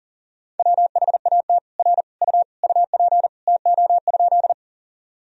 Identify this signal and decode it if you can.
Morse code, heard as a keyed tone, about 40 wpm, 700 Hz. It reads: WHAT R U UP TO?
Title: Sentences from Top 100 Words and Abbreviations 40wpm